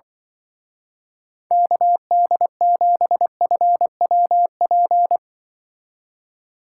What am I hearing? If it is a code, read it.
Morse code, heard as a keyed tone, about 24 wpm, 700 Hz. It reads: KD7FWP
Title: Speed Racing - USA 2x3 Call Signs 20wpm